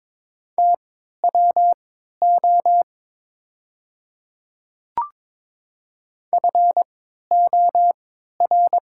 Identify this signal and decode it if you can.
Morse code, heard as a keyed tone, about 22 wpm, spaced as for 12 wpm, 700 Hz. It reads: TWO FOR
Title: 3x Character Spacing - Top 100 Words - 22wpm